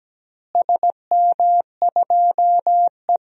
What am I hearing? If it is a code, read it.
Morse code, heard as a keyed tone, about 17 wpm, 700 Hz. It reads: SM2E
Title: DXCC-100 Call Signs - 4 Characters - Repeated 2x - 17wpm